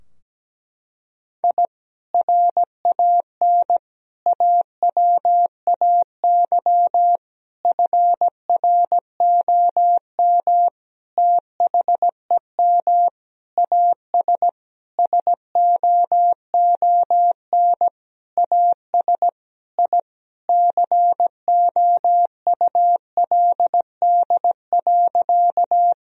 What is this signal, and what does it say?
Morse code, heard as a keyed tone, about 17 wpm, 700 Hz. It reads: I RAN AWAY FROM THEM AS SOON AS I COULD.
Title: Sentences from Top 1000 Words - Review Part 3 - 17wpm